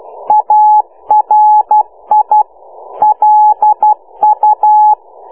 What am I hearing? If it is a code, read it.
Morse code, heard as a keyed tone, about 12 wpm, 850 Hz. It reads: ARI LU